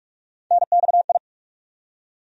Morse code, heard as a keyed tone, about 46 wpm, 700 Hz. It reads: NXI
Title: USA Call Sign Postfixes - 3 Letters 45wpm